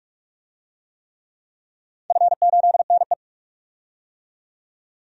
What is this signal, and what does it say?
Morse code, heard as a keyed tone, about 45 wpm, 700 Hz. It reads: F8NE
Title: DXCC-100 Call Signs - 4 Characters - 45wpm